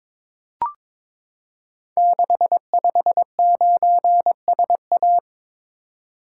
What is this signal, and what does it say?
Morse code, heard as a keyed tone, about 22 wpm, 700 Hz. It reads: 659SA
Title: Code Groups - 5 Letter:Numbers - 22wpm